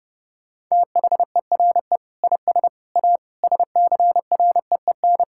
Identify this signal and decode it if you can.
Morse code, heard as a keyed tone, about 30 wpm, 700 Hz. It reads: THERE IS A SCREEN
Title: Sentences from Top 1000 Words - Review Part 6 - 30wpm